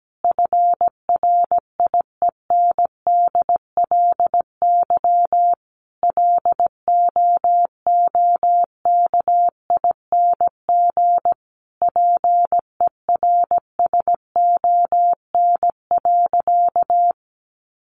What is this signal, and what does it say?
Morse code, heard as a keyed tone, about 17 wpm, 700 Hz. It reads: FRIENDLY LOOKING PERSON.
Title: Easy Phrases 17wpm